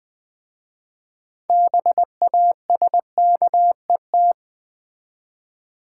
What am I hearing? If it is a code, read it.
Morse code, heard as a keyed tone, about 20 wpm, 700 Hz. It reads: BASKET
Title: Easy Words with Hints - 20wpm